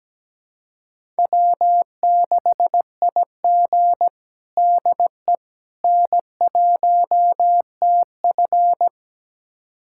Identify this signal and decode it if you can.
Morse code, heard as a keyed tone, about 17 wpm, 700 Hz. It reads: W6IG DE N1TF
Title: USA Callsign DE Callsign - 4 Characters - 17wpm